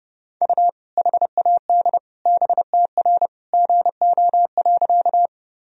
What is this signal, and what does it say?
Morse code, heard as a keyed tone, about 32 wpm, 700 Hz. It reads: U HAD BTR GO.
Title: Sentences from Top 400 Words and Abbreviations 30wpm